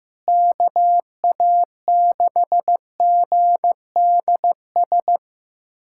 Morse code, heard as a keyed tone, about 15 wpm, 700 Hz. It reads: KA6GDS
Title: USA Callsign DE Callsign - 6 Characters - 15wpm